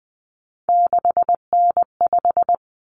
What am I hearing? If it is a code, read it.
Morse code, heard as a keyed tone, about 20 wpm, 700 Hz. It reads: 6N5
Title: Common Cut Numbers - 3-Digit 20wpm